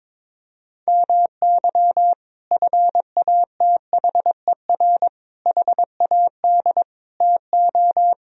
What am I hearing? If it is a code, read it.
Morse code, heard as a keyed tone, about 22 wpm, 700 Hz. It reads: MY FATHER HAD TO
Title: Sentences from Top 1000 Words - Review Part 1 - 22wpm